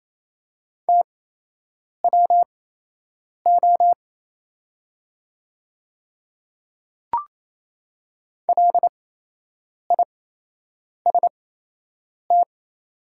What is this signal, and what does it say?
Morse code, heard as a keyed tone, about 28 wpm, spaced as for 8 wpm, 700 Hz. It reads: TWO LIST
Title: 8x Character Spacing - Top 500 Words - 28wpm